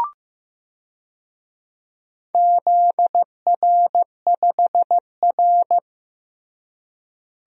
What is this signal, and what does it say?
Morse code, heard as a keyed tone, about 16 wpm, 700 Hz. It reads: ZR5R
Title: DXCC-100 Call Signs - 4 Characters - Repeated 3x - 15wpm